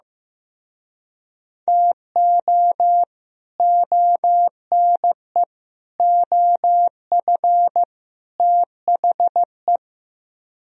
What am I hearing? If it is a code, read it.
Morse code, heard as a keyed tone, about 16 wpm, 700 Hz. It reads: TO ONE OF THE
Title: Speed-Racing - Sets of 4 Words 15wpm